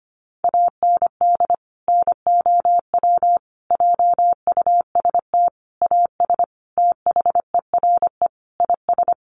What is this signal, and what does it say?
Morse code, heard as a keyed tone, about 25 wpm, 700 Hz. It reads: AND NOW JUST AS THERE IS
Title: Sentences from Top 1200 Words 25wpm